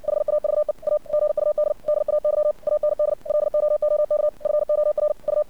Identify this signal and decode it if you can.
Morse code, heard as a keyed tone, about 59 wpm, 600 Hz. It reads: HAVE A GUD DAY AND LOOK FOR U
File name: archived / CRSnet